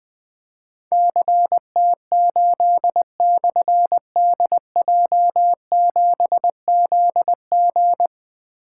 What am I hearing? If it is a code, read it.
Morse code, heard as a keyed tone, about 20 wpm, 700 Hz. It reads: CT8/DJ7ZG
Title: Difficult Call Signs 20wpm